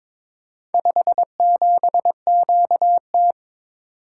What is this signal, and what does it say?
Morse code, heard as a keyed tone, about 21 wpm, 700 Hz. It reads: 57QT
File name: practice / 22wpm / Four Letter-Number Combinations - 22wpm